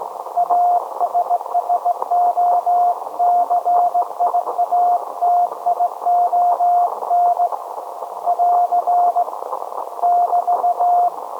Morse code, heard as a keyed tone, about 18 wpm, 710 Hz. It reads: ASSOCIATION <AR> <BT>
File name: storage / message Grimeton SAQ